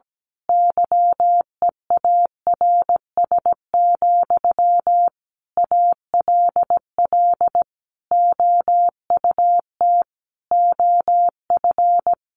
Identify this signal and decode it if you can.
Morse code, heard as a keyed tone, about 17 wpm, 700 Hz. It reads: YEARS, ALL OUT OF
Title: A Sound Of Thunder 17wpm